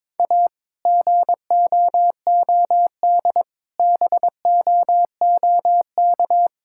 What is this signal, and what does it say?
Morse code, heard as a keyed tone, about 22 wpm, 700 Hz. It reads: A GOOD BOOK
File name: practice / 22wpm / Sentences with 2-Word Phrases - 22wpm